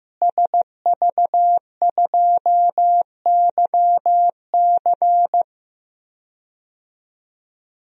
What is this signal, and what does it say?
Morse code, heard as a keyed tone, about 15 wpm, 700 Hz. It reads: SV2YC
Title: DXCC-100 Call Signs - 5 Characters - Repeated 3x - 15wpm